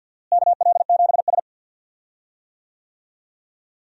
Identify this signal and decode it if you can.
Morse code, heard as a keyed tone, about 52 wpm, 700 Hz. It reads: KL7S
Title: DXCC-100 Call Signs - 4 Characters - Repeated 3x - 50wpm